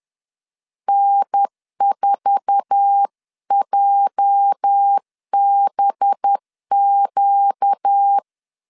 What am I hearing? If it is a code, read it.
Morse code, heard as a keyed tone, about 10 wpm, 800 Hz. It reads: N4JBQ
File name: LEZIONI / 9 2867 10wpm 2733